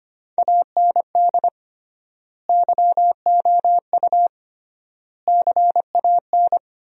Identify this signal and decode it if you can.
Morse code, heard as a keyed tone, about 25 wpm, 700 Hz. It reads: AND YOU CAN
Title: 3x Word-Spacing - Sentences from Top 400 Words - 25wpm